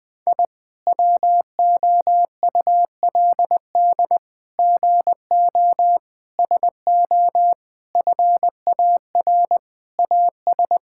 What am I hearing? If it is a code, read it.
Morse code, heard as a keyed tone, about 20 wpm, 700 Hz. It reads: I WOULD GO SO FAR AS
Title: Sets of 8 Words - 20wpm